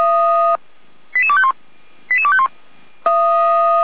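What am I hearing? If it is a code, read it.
Five-tone selective call (ZVEI), heard as quick runs of five tones, tones of about 70 ms. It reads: 88251 88251
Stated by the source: ZVEI five-tone selective call 88352 received by SDR.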